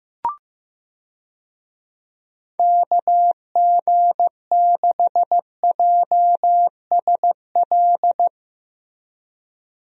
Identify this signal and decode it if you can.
Morse code, heard as a keyed tone, about 15 wpm, 700 Hz. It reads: KG6JSL
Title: USA Call Signs - 6 Characters - Repeated 3x - 15wpm